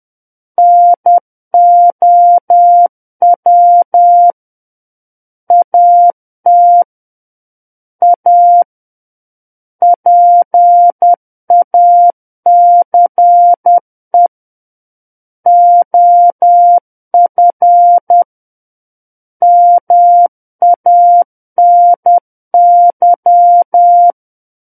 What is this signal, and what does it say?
Morse code, heard as a keyed tone, about 10 wpm, 700 Hz. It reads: NOW AT A PACE OF MANY
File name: Book One / 01-Chapter 1 - 10 WPM